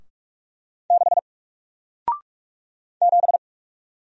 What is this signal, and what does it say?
Morse code, heard as a keyed tone, about 45 wpm, 700 Hz. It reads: 6 7
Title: Single Character (Letter:Number:Punctuation:Prosign) Alt 45wpm